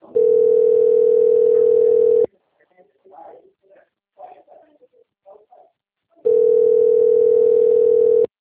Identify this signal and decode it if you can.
Ringback tone (North America), 440 + 480 Hz, with speech.
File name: out- - - - - .35116